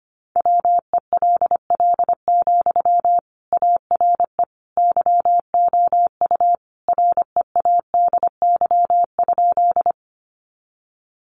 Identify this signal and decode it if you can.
Morse code, heard as a keyed tone, about 25 wpm, 700 Hz. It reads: WELL, ARE YOU READY?
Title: Sentences from Top 400 Words 25wpm